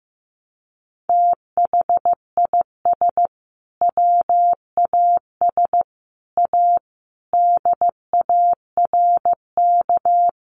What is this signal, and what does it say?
Morse code, heard as a keyed tone, about 15 wpm, 700 Hz. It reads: THIS WAS A DARK
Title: Sentences from Top 1300 Words 15wpm